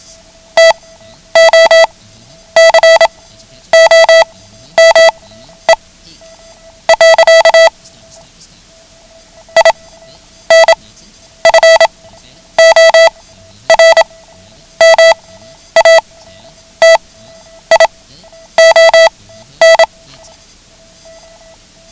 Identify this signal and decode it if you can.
Morse code, heard as a keyed tone, about 27 wpm, spaced as for 10 wpm, 660 Hz. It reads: TO COME. INFORMATION
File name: ham radio / CodeQuick / Code Quick 09